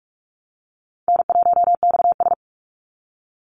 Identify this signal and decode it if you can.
Morse code, heard as a keyed tone, about 44 wpm, 700 Hz. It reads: N1XS
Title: Easy Call Signs Repeated 2x - 45wpm